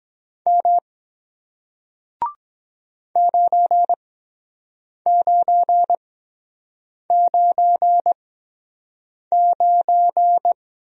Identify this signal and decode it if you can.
Morse code, heard as a keyed tone, about 23 wpm, spaced as for 7 wpm, 700 Hz. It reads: M 9999